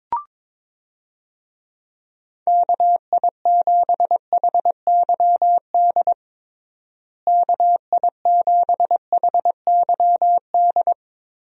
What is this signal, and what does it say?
Morse code, heard as a keyed tone, about 22 wpm, 700 Hz. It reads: KI7HYD KI7HYD
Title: USA Call Signs - 6 Characters - Repeated 3x - 22wpm